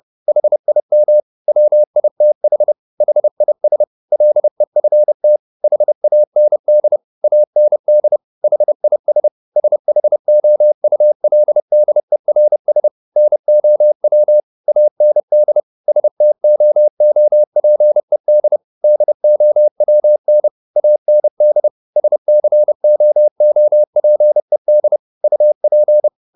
Morse code, heard as a keyed tone, about 30 wpm, 600 Hz. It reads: HIM WITH HIS LEFT HAND AND HIS SHOULDERS NOW AND STOOPED DOWN AND SCOOPED UP